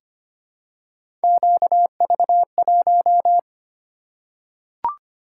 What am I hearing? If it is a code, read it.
Morse code, heard as a keyed tone, about 25 wpm, 700 Hz. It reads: QV1
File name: Three Letter-Number Combinations - 25wpm